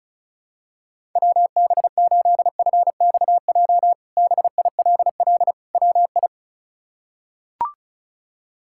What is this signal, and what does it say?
Morse code, heard as a keyed tone, about 36 wpm, 700 Hz. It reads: WB8FXJ BILL WI
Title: K1USN SST QSO Practice - 35wpm